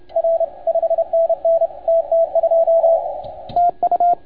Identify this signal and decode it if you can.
Morse code, heard as a keyed tone, about 28 wpm, 690 Hz, with radio noise. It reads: R 5NN TT2 TU
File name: Archiv / CUC287 / OK1FGD CUC 287